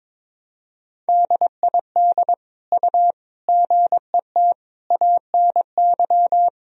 Morse code, heard as a keyed tone, about 22 wpm, 700 Hz. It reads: DID U GET ANY
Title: Sentences from Top 400 Words and Abbreviations 22wpm